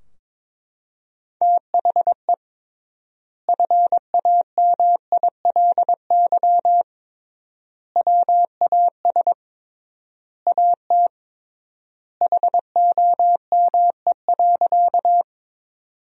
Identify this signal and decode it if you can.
Morse code, heard as a keyed tone, about 22 wpm, 700 Hz. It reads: THE FAMILY WAS AT HOME.
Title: 3x Word-Spacing - Sentences from Top 400 Words - 22wpm